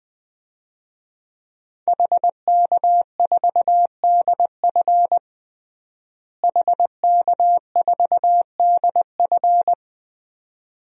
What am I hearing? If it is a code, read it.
Morse code, heard as a keyed tone, about 20 wpm, 700 Hz. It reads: HK4DF HK4DF
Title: DXCC-100 Call Signs - 5 Characters - Repeated 2x - 20wpm